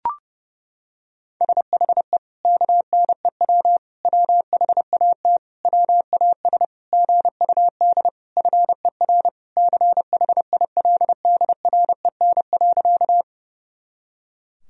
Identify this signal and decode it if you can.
Morse code, heard as a keyed tone, about 30 wpm, 700 Hz. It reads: SHE KNEW WHAT WAS GUD FER CHILDREN.
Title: Sentences from Top 400 Words and Abbreviations 30wpm